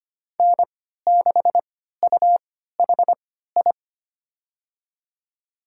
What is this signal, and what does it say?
Morse code, heard as a keyed tone, about 25 wpm, spaced as for 14 wpm, 700 Hz. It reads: N6UHI
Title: USA Call Signs - 5 Characters - 3x Character Spacing - 25wpm